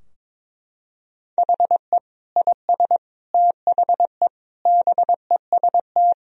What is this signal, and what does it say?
Morse code, heard as a keyed tone, about 22 wpm, 700 Hz. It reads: HE IS THE BEST
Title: Sentences from Top 1000 Words - Review Part 4 - 22wpm